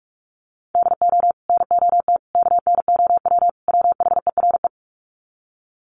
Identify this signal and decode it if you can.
Morse code, heard as a keyed tone, about 44 wpm, 700 Hz. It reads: DO NOT KNOW WHERE
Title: Sets of 4 Words 45wpm